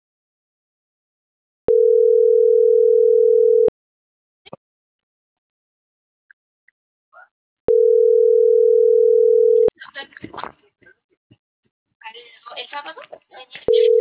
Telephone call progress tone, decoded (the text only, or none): ringback tone